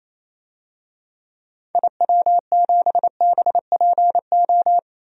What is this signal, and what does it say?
Morse code, heard as a keyed tone, about 28 wpm, 700 Hz. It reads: IW7BPO
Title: DXCC-100 Call Signs - 6 Characters - 28wpm